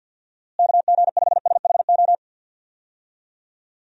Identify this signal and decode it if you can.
Morse code, heard as a keyed tone, about 50 wpm, 700 Hz. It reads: KG5SHO
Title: USA Call Signs - 6 Characters - Repeated 2x - 50wpm